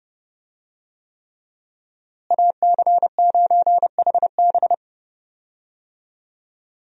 Morse code, heard as a keyed tone, about 30 wpm, 700 Hz. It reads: AC9HB
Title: USA Call Signs - 5 Characters - Repeated 3x - 30wpm